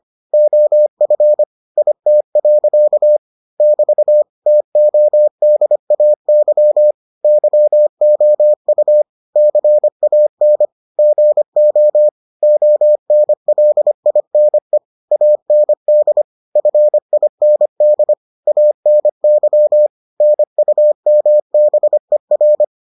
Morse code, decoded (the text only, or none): OF IT. <BT> TODAY YOU CAN GO ONLINE AND FIND ANY NUMBER